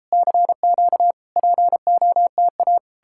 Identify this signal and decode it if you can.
Morse code, heard as a keyed tone, about 39 wpm, 700 Hz. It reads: CQ POTA